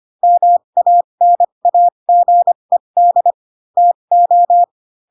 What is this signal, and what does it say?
Morse code, heard as a keyed tone, about 25 wpm, spaced as for 22 wpm, 700 Hz. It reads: MANAGED TO
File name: Pinocchio / Pinocchio ch6 18 25 ik1hge